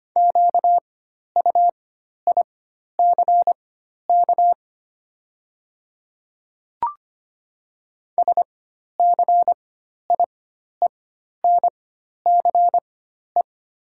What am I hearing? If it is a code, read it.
Morse code, heard as a keyed tone, about 25 wpm, spaced as for 12 wpm, 700 Hz. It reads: QUICK SCIENCE
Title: 4x Character Spacing - Top 500 Words - 25wpm